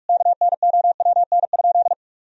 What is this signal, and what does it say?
Morse code, heard as a keyed tone, about 46 wpm, 700 Hz. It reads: KNOWN?
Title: Sentences from Top 1000 Words - Review Part 3 - 45wpm